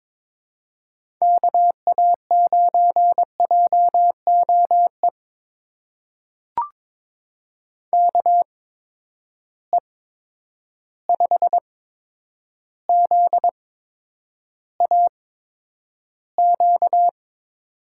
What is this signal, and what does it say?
Morse code, heard as a keyed tone, about 22 wpm, spaced as for 9 wpm, 700 Hz. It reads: KA9JOE KE5ZAQ
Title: USA Call Signs - 6 Characters - 8x Character Spacing - 22wpm